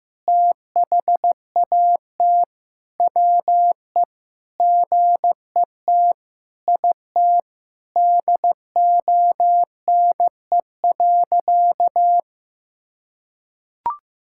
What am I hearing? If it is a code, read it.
Morse code, heard as a keyed tone, about 15 wpm, 700 Hz. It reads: THAT WE GET IT DONE.
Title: Sentences from Top 1600 Words 15wpm